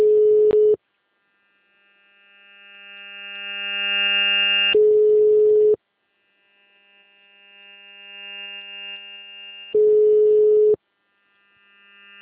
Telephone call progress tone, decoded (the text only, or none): ringback tone